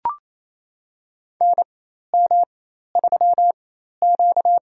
Morse code, heard as a keyed tone, about 28 wpm, spaced as for 13 wpm, 700 Hz. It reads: NM3Q
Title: USA Call Signs - 4 Characters - 4x Character Spacing - 28wpm